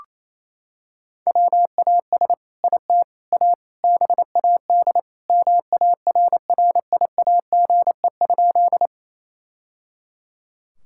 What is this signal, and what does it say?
Morse code, heard as a keyed tone, about 28 wpm, 700 Hz. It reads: WAS IT A BAD MARRIAGE?
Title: Sentences from Top 1000 Words - Review Part 5 - 28wpm